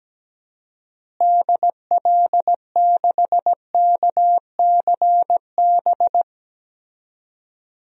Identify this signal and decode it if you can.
Morse code, heard as a keyed tone, about 17 wpm, 700 Hz. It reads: DL6KCB
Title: DXCC-100 Call Signs - 6 Characters - Repeated 3x - 17wpm